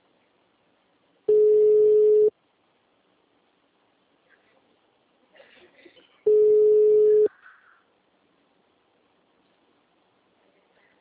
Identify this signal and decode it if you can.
Ringback tone (Europe, 425 Hz), with speech.
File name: out- -2006- - - .54663